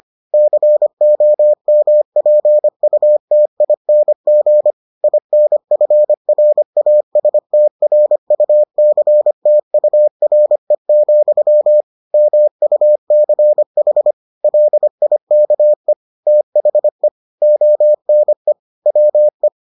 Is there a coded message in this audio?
Yes: Morse code from 0.3 to 19.5 s, heard as a keyed tone, about 25 wpm, 600 Hz. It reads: COMPUTING INFRASTRUCTURE, MUCH LIKE THE ONE WE